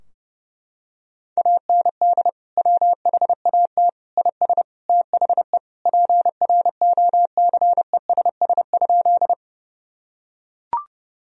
Morse code, heard as a keyed tone, about 30 wpm, 700 Hz. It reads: AND WHAT IS THE PROCESS?
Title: Sentences from Top 1000 Words - Review Part 3 - 30wpm